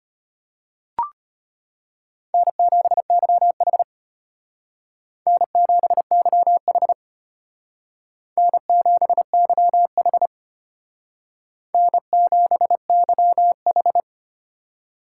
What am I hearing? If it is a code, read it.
Morse code, heard as a keyed tone, about 33 wpm, 700 Hz. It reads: N7YH N7YH N7YH N7YH